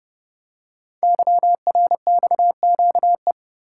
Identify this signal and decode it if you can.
Morse code, heard as a keyed tone, about 29 wpm, 700 Hz. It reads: YRXQE
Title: Code Groups - 5 Letters - 30wpm